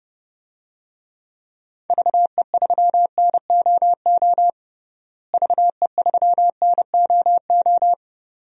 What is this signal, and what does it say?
Morse code, heard as a keyed tone, about 30 wpm, 700 Hz. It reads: VE3NOO VE3NOO
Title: DXCC-100 Call Signs - 6 Characters - Repeated 2x - 30wpm